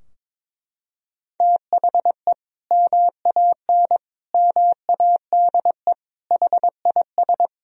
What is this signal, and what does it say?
Morse code, heard as a keyed tone, about 22 wpm, 700 Hz. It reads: THE MAN MADE HIS
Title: Sentences from Top 1000 Words - Review Part 2 - 22wpm